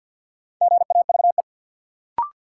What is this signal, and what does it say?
Morse code, heard as a keyed tone, about 53 wpm, 700 Hz. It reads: GAVE
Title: Sentences from Top 500 Words - New Word Drill - 50wpm